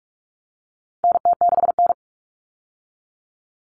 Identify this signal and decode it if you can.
Morse code, heard as a keyed tone, about 44 wpm, 700 Hz. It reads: NT6N
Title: All Cut Numbers - 4-Digit 45wpm